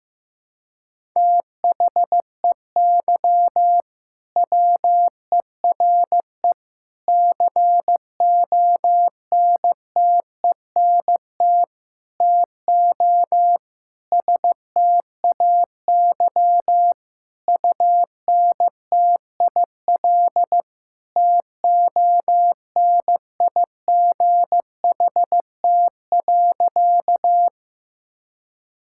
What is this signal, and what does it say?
Morse code, heard as a keyed tone, about 15 wpm, 700 Hz. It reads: THEY WERE CONTENT TO STAY UNTIL TONIGHT.
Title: Sentences from Top 1000 Words - Review Part 6 - 15wpm